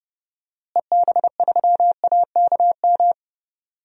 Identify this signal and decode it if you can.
Morse code, heard as a keyed tone, about 30 wpm, 700 Hz. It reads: EB3AKM
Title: DXCC-100 Call Signs - 6 Characters - Repeated 3x - 30wpm